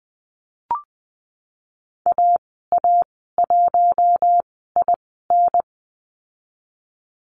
Morse code, heard as a keyed tone, about 20 wpm, spaced as for 15 wpm, 700 Hz. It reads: AA1IN
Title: 069 Lesson 12g - Letter d - Call Signs 20wpm